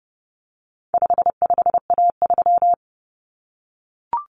Morse code, heard as a keyed tone, about 30 wpm, 700 Hz. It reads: 55A3
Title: Common Cut Numbers - 4-Digit 30wpm